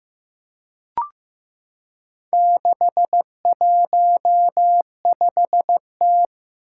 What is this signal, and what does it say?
Morse code, heard as a keyed tone, about 15 wpm, 700 Hz. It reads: 615T
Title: Four Challenging Letter-Number Combinations 15wpm